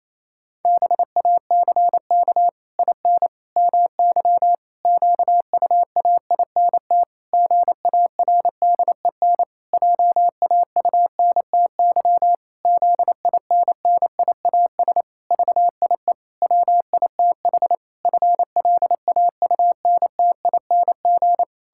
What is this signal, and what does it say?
Morse code, heard as a keyed tone, about 28 wpm, 700 Hz. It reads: BACK IN MY QUAINT GARDEN JAUNTY ZINNIAS VIE WITH FLAUNTING
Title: Pangram Challenge v2 - 28wpm